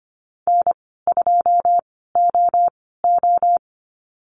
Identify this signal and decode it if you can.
Morse code, heard as a keyed tone, about 25 wpm, spaced as for 16 wpm, 700 Hz. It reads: N2OO
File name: Lesson 15g - Number 5 - Call Signs 25wpm